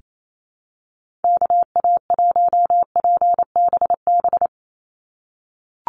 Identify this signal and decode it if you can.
Morse code, heard as a keyed tone, about 28 wpm, 700 Hz. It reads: KA1PBB
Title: USA Call Signs 28wpm